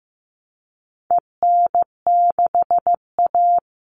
Morse code, heard as a keyed tone, about 15 wpm, 700 Hz. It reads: EN6A
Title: All Cut Numbers - 4-Digit 15wpm